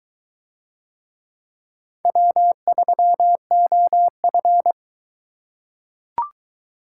Morse code, heard as a keyed tone, about 22 wpm, 700 Hz. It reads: W3OF